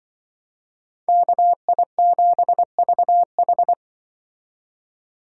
Morse code, heard as a keyed tone, about 25 wpm, 700 Hz. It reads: KI7VH
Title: Speed Racing - USA 2x2 Call Signs 20wpm